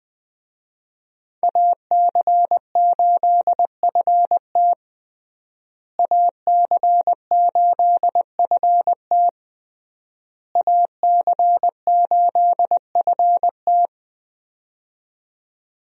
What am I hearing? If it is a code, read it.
Morse code, heard as a keyed tone, about 20 wpm, 700 Hz. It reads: AC8FT AC8FT AC8FT
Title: USA Call Signs - 5 Characters - Repeated 3x - 20wpm